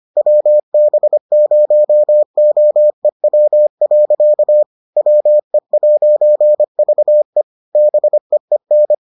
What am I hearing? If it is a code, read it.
Morse code, heard as a keyed tone, about 25 wpm, 600 Hz. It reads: WB0OEW. WE'VE BEEN